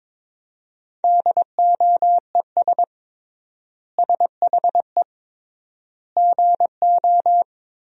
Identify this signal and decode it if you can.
Morse code, heard as a keyed tone, about 22 wpm, 700 Hz. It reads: DOES SHE GO
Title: 3x Word-Spacing - Sentences from Top 400 Words - 22wpm